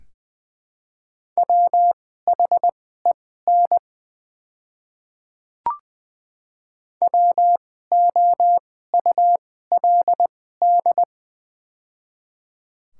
Morse code, heard as a keyed tone, about 21 wpm, spaced as for 15 wpm, 700 Hz. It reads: WHEN WOULD
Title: 2x Character Spacing - Top 100 Words - 20wpm